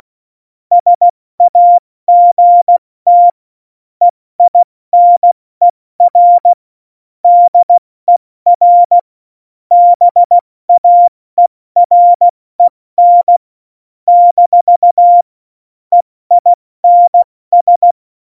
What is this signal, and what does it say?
Morse code, heard as a keyed tone, about 16 wpm, 700 Hz. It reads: SAGT EINER DER BAEREN - EINS
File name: Morsekurs / Witze / Witz 29. Mai 2018, 70 F80